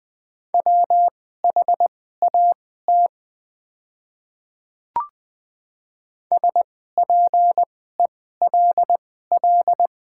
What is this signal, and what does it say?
Morse code, heard as a keyed tone, about 20 wpm, spaced as for 14 wpm, 700 Hz. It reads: WHAT SPELL
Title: 2x Character Spacing - Top 200 Words - 20wpm